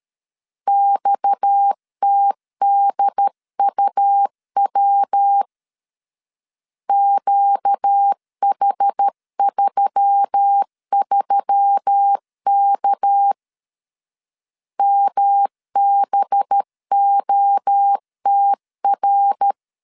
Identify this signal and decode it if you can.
Morse code, heard as a keyed tone, about 13 wpm, 800 Hz. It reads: XTDUW QH33K MBOTR
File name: CW / DET / 1 6 1500C S C 12wpm 3533